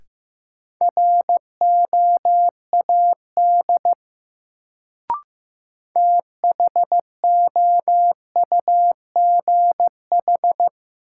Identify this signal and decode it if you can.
Morse code, heard as a keyed tone, about 15 wpm, 700 Hz. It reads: ROAD THOUGH
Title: Sentences from Top 400 Words - New Word Drill - 15wpm